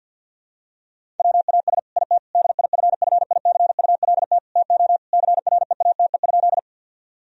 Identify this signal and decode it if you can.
Morse code, heard as a keyed tone, about 50 wpm, 700 Hz. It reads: WAS IT DIFFICULT TO CREATE?
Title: Sentences from Top 1000 Words - Review Part 3 - 50wpm